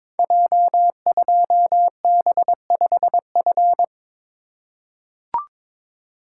Morse code, heard as a keyed tone, about 23 wpm, 700 Hz. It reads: J2B5F
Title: Code Groups - 5 Letter:Numbers - 22wpm